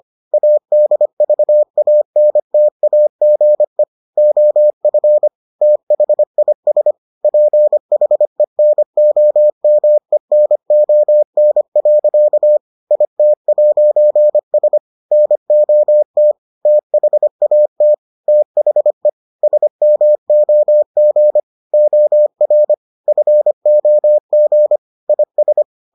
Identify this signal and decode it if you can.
Morse code, heard as a keyed tone, about 25 wpm, 600 Hz. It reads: ADVANTAGE OF THIS PHENOMENON. IT'S NOT THAT THE SMOG OR FOG IS